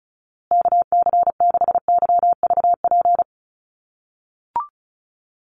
Morse code, heard as a keyed tone, about 34 wpm, 700 Hz. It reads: KC6YVP
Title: USA Call Signs Repeated 3x 35wpm